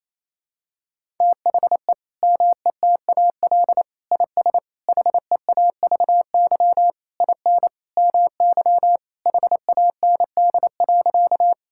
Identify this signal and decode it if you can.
Morse code, heard as a keyed tone, about 28 wpm, 700 Hz. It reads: THE METAL IS HEAVY IN MY HAND.
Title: Sentences from Top 1000 Words - Review Part 6 - 28wpm